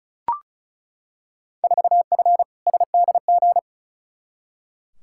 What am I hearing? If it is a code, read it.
Morse code, heard as a keyed tone, about 36 wpm, 700 Hz. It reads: 4F SDG
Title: Field Day Exchanges - 35wpm